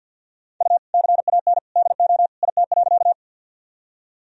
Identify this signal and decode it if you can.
Morse code, heard as a keyed tone, about 53 wpm, 700 Hz. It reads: U CAN DO IT.
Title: Sentences from Top 100 Words and Abbreviations 50wpm